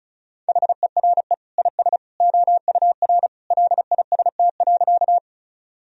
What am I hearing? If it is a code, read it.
Morse code, heard as a keyed tone, about 36 wpm, 700 Hz. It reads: HERE IS OUR LIST.
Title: Sentences from Top 1000 Words - Review Part 2 - 35wpm